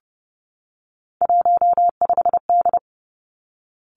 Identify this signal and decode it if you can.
Morse code, heard as a keyed tone, about 30 wpm, 700 Hz. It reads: 15D
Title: Three Challenging Letter-Number Combinations 30wpm